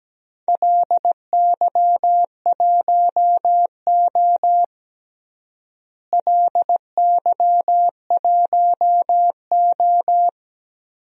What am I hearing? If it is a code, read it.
Morse code, heard as a keyed tone, about 17 wpm, 700 Hz. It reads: LY1O LY1O
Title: DXCC-100 Call Signs - 4 Characters - Repeated 3x - 17wpm